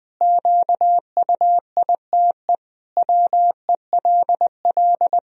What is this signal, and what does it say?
Morse code, heard as a keyed tone, about 21 wpm, 700 Hz. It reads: QUITE WELL
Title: Sentences from Top 1000 Words - Review Part 3 - 20wpm